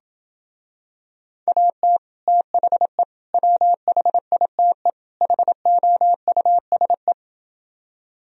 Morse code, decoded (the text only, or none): AT THE WHITE HOUSE